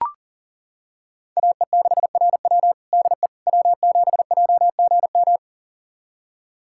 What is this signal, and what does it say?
Morse code, heard as a keyed tone, about 40 wpm, 700 Hz. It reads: AE6RW DE W7JGM
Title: USA Callsign DE Callsign - 5 Characters - 40wpm